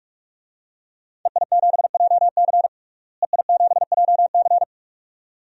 Morse code, heard as a keyed tone, about 46 wpm, 700 Hz. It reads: EI7JC EI7JC
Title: DXCC-100 Call Signs - 5 Characters - Repeated 2x - 45wpm